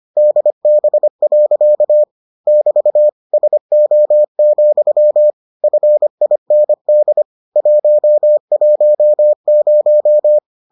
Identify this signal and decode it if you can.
Morse code, heard as a keyed tone, about 25 wpm, 600 Hz. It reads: DB. <BT> SO, FIND 110